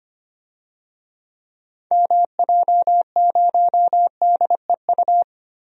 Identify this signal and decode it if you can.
Morse code, heard as a keyed tone, about 25 wpm, 700 Hz. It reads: MJ0DEU
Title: DXCC-100 Call Signs - 6 Characters - Repeated 2x - 25wpm